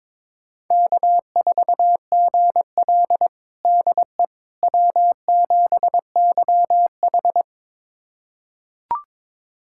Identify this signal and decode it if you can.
Morse code, heard as a keyed tone, about 22 wpm, 700 Hz. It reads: K4GL DE W7YH
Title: USA Callsign DE Callsign - 4 Characters - 22wpm